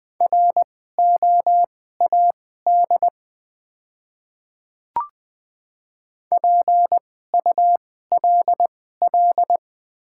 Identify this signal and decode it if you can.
Morse code, heard as a keyed tone, about 20 wpm, spaced as for 14 wpm, 690 Hz. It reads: ROAD PULL
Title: 2x Character Spacing - Top 500 Words - 20wpm